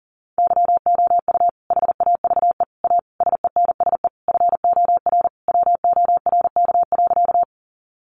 Morse code, heard as a keyed tone, about 40 wpm, 700 Hz. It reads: YOU HAVE A SENSE FOR WORK.
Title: Sentences from Top 700 Words 40wpm